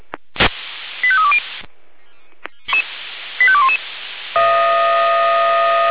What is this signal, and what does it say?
Five-tone selective call (ZVEI), heard as quick runs of five tones, tones of about 70 ms. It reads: 85320 85211
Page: /496/SDR/ZVEI/85340/